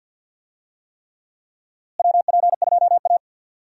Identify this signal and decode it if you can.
Morse code, heard as a keyed tone, about 48 wpm, 700 Hz. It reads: WP2A
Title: USA Call Signs - 4 Characters - Repeated 3x - 50wpm